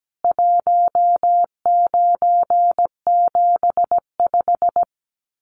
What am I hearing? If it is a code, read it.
Morse code, heard as a keyed tone, about 17 wpm, 700 Hz. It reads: 1975
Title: Common QSO Phrases 17wpm